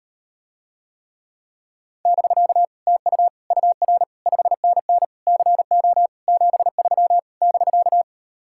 Morse code, heard as a keyed tone, about 39 wpm, 700 Hz. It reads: <BK> TU UR 5NN CO 73 <BK>